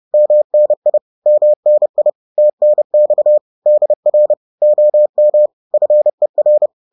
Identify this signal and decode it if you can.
Morse code, heard as a keyed tone, about 30 wpm, 600 Hz. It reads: MNI MNI TNX DR OM FER
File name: 30wpm-7